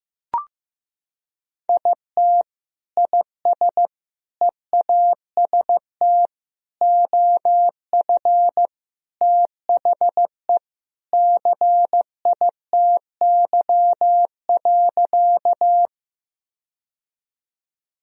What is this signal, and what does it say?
Morse code, heard as a keyed tone, about 15 wpm, 700 Hz. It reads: IT IS EAST OF THE CITY.
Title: Sentences from Top 1000 Words - Review Part 3 - 15wpm